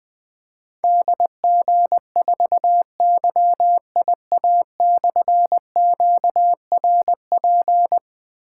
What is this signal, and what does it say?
Morse code, heard as a keyed tone, about 20 wpm, 700 Hz. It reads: DG4YIA/QRP
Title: Difficult Call Signs 20wpm